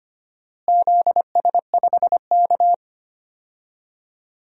Morse code, heard as a keyed tone, about 25 wpm, 700 Hz. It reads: ZS5K
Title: DXCC-100 Call Signs - 4 Characters - Repeated 3x - 25wpm